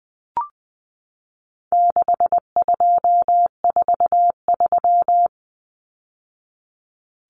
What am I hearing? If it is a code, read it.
Morse code, heard as a keyed tone, about 20 wpm, 700 Hz. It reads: 6243
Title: Numbers - 4-Digit 20wpm